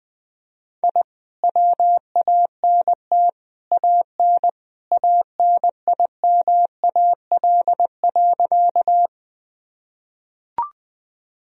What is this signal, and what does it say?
Morse code, heard as a keyed tone, about 21 wpm, 700 Hz. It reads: I WANT AN ANIMAL.
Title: Sentences from Top 1000 Words - Review Part 1 - 20wpm